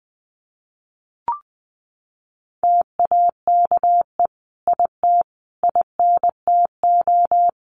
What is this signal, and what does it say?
Morse code, heard as a keyed tone, about 20 wpm, 700 Hz. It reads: TAKE IT INTO
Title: Sentences from Top 1500 Words 20wpm